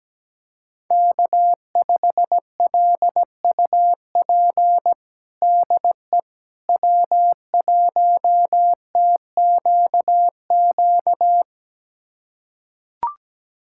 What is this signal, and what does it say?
Morse code, heard as a keyed tone, about 17 wpm, 700 Hz. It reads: K5LUP DE W1TQQ
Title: USA Callsign DE Callsign - 5 Characters - 17wpm